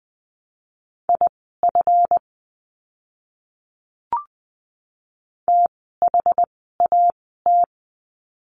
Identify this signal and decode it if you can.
Morse code, heard as a keyed tone, about 20 wpm, spaced as for 14 wpm, 700 Hz. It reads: IF THAT
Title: 199 Lesson 32f - Letter k - Words 20wpm